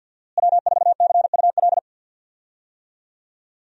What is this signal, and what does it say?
Morse code, heard as a keyed tone, about 50 wpm, 700 Hz. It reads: W4CUL
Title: USA Call Signs - 5 Characters - Repeated 2x - 50wpm